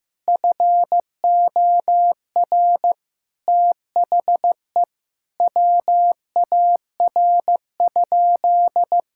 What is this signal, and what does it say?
Morse code, heard as a keyed tone, about 15 wpm, 700 Hz. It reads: FOR THE WAR?
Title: Sentences from Top 1000 Words - Review Part 5 - 15wpm